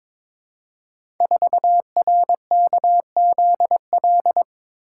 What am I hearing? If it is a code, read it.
Morse code, heard as a keyed tone, about 23 wpm, 700 Hz. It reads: 4RKZL
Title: Code Groups - 5 Letter:Numbers - 22wpm